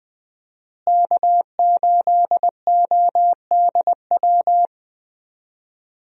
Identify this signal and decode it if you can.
Morse code, heard as a keyed tone, about 20 wpm, 700 Hz. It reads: K8ODW
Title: USA Call Signs - 5 Characters - Repeated 3x - 20wpm